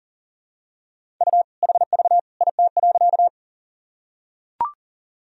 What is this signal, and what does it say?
Morse code, heard as a keyed tone, about 40 wpm, 700 Hz. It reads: U HV IT.
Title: Sentences from Top 100 Words and Abbreviations 40wpm